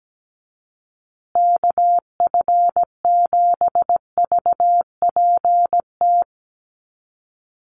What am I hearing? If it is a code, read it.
Morse code, heard as a keyed tone, about 17 wpm, 700 Hz. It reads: KF7VPT
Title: USA Call Signs Repeated 3x 17wpm